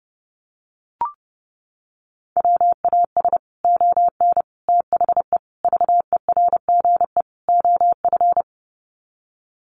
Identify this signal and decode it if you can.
Morse code, heard as a keyed tone, about 30 wpm, 700 Hz. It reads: WAS ON THE VERGE OF
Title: Sets of 5 Words Encore 30wpm